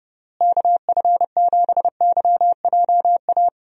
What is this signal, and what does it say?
Morse code, heard as a keyed tone, about 30 wpm, 700 Hz. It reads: KF7YJA
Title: USA Call Signs - 6 Characters - Repeated 3x - 30wpm